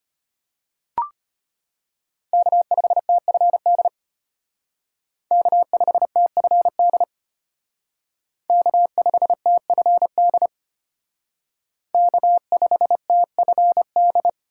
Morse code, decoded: K5TFD K5TFD K5TFD K5TFD

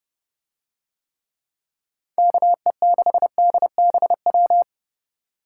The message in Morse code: KE6DBW

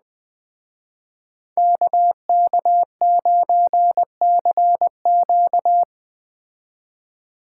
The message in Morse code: KK9CQ

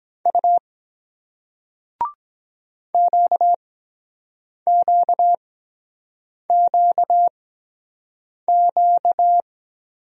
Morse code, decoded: U QQQQ

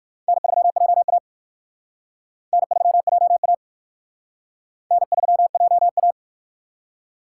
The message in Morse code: N3JA N3JA N3JA

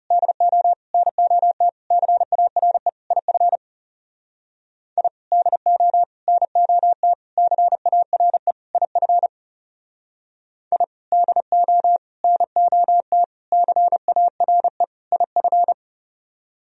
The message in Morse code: DO NOT CARE IF I DO NOT CARE IF I DO NOT CARE IF

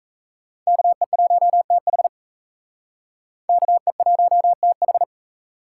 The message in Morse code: KE1TH KE1TH

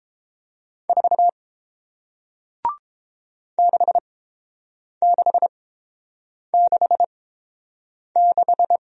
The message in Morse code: 4 6666

